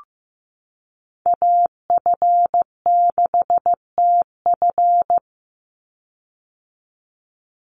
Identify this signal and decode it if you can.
Morse code, heard as a keyed tone, about 15 wpm, 700 Hz. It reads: AF6TF
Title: USA Call Signs 15wpm